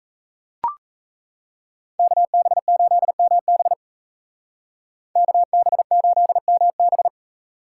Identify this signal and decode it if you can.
Morse code, heard as a keyed tone, about 33 wpm, 700 Hz. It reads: KB8MB KB8MB
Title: Speed Racing - USA 2x2 Call Signs 28wpm